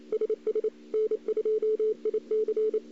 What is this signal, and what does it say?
Morse code, heard as a keyed tone, about 28 wpm, 430 Hz. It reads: SS N2IC
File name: CW / 2008-ss-cw-k5tr-op-n5rz -02:00